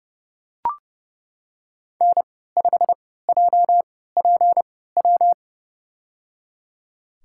Morse code, heard as a keyed tone, about 30 wpm, spaced as for 17 wpm, 700 Hz. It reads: N5JPW
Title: USA Call Signs - 5 Characters - 3x Character Spacing - 30wpm